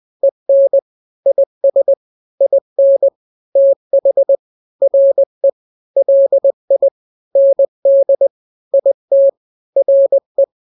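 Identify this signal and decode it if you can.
Morse code, heard as a keyed tone, about 20 wpm, 550 Hz. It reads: EN IS IN TH RE LI ND IT RE
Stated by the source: Morse drill of two-letter n-grams at 20/18 WPM (Farnsworth), from KK7RR.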